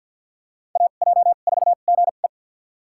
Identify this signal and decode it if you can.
Morse code, heard as a keyed tone, about 52 wpm, spaced as for 36 wpm, 700 Hz. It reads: AJ4GE
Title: USA Call Signs - 5 Characters - 2x Character Spacing - 50wpm